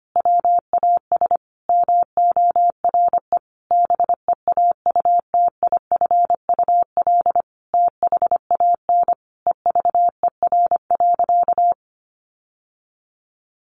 Morse code, heard as a keyed tone, about 25 wpm, 700 Hz. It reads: WAS MORE BEAUTIFUL THAN EVER.
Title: Sentences from Top 1200 Words 25wpm